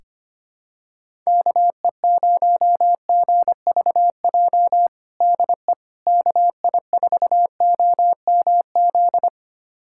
Morse code, heard as a keyed tone, about 25 wpm, 700 Hz. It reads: KE0GVJ DE KI4OMZ
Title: USA Callsign DE Callsign 25wpm